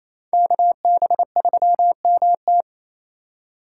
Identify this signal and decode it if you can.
Morse code, heard as a keyed tone, about 28 wpm, 700 Hz. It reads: KB3MT
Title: USA Call Signs - 8x Character Spacing - 28wpm